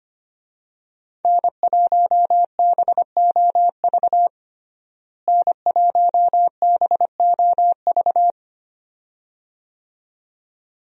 Morse code, heard as a keyed tone, about 25 wpm, 700 Hz. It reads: N1BOV N1BOV
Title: USA Call Signs - 5 Characters - Repeated 2x - 25wpm